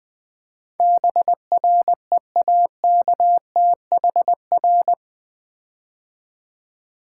Morse code, heard as a keyed tone, about 20 wpm, 700 Hz. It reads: BREAKTHR
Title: Build A Word - 4 Letters At A Time - 12-Letter Words - 20wpm